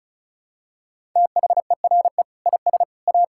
Morse code, heard as a keyed tone, about 36 wpm, 700 Hz. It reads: THERE IS A
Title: Sentences from Top 1000 Words - Review Part 6 - 35wpm